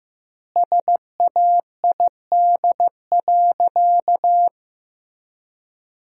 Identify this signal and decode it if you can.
Morse code, heard as a keyed tone, about 15 wpm, 700 Hz. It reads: SAID.
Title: 2x Word-Spacing - Sentences from Top 400 Words - 15wpm